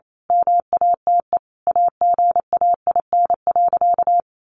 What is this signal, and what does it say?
Morse code, heard as a keyed tone, about 28 wpm, 700 Hz. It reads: MATE AGAIN.
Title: A Sound Of Thunder 28wpm